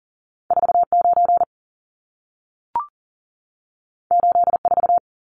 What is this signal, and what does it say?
Morse code, heard as a keyed tone, about 40 wpm, 700 Hz. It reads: 49 84
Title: Numbers - 2-Digit 40wpm